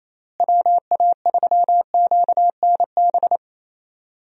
Morse code, heard as a keyed tone, about 27 wpm, 700 Hz. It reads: WA3QNB
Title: USA Call Signs - 6 Characters - Repeated 2x - 28wpm